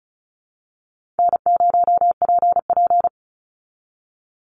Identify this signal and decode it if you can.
Morse code, heard as a keyed tone, about 34 wpm, 700 Hz. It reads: N0PP
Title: Call Signs - 35wpm